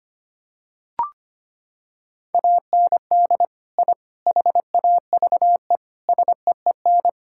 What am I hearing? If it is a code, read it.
Morse code, heard as a keyed tone, about 25 wpm, 700 Hz. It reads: AND I HAVE SEEN
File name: practice / 25wpm / Sentences with 3-Word Phrases - 25wpm